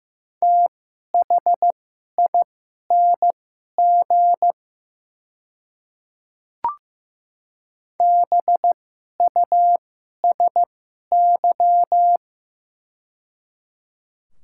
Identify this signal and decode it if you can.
Morse code, heard as a keyed tone, about 15 wpm, spaced as for 11 wpm, 690 Hz. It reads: THING BUSY
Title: 2x Character Spacing - Top 500 Words - 15wpm